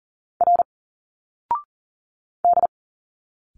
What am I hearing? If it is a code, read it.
Morse code, heard as a keyed tone, about 40 wpm, 700 Hz. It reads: R D
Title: Single Letters 40wpm